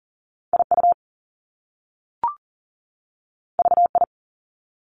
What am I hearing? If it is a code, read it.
Morse code, heard as a keyed tone, about 40 wpm, 700 Hz. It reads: IU VI
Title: Top Two Letter Combinations in Words 40wpm